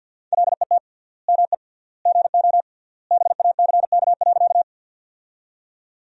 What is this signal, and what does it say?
Morse code, heard as a keyed tone, about 46 wpm, 700 Hz. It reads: LET ME GO BACK.
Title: 3x Word-Spacing - Sentences from Top 300 Words - 50wpm